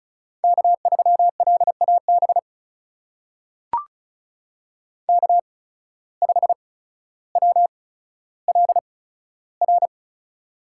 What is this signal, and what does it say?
Morse code, heard as a keyed tone, about 36 wpm, spaced as for 15 wpm, 700 Hz. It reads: K3LAB K5WLR
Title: USA Call Signs - 5 Characters - 8x Character Spacing - 35wpm